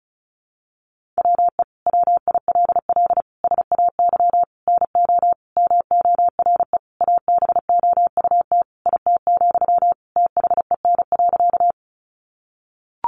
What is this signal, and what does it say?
Morse code, heard as a keyed tone, about 34 wpm, 700 Hz. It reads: WE WILL SAY NO MORE ABOUT IT, THEN.
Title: Sentences from Top 200 Words 35wpm